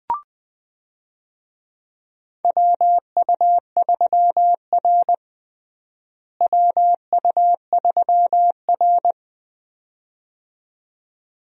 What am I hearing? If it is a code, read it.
Morse code, heard as a keyed tone, about 21 wpm, 700 Hz. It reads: WU3R WU3R
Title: USA Call Signs - 4 Characters - Repeated 2x - 20wpm